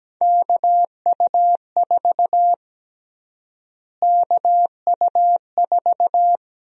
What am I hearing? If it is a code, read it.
Morse code, heard as a keyed tone, about 17 wpm, 700 Hz. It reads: KU4 KU4
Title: USA Call Sign Prefixes - Repeated 2x - 17wpm